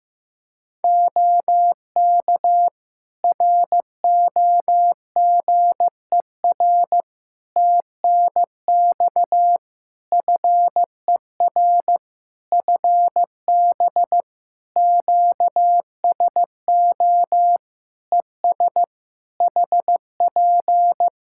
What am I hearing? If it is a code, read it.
Morse code, heard as a keyed tone, about 15 wpm, 700 Hz. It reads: OK ROGER TNX FER FB QSO ES HP
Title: Practice QSO - Exchange 4 - Ending - 15wpm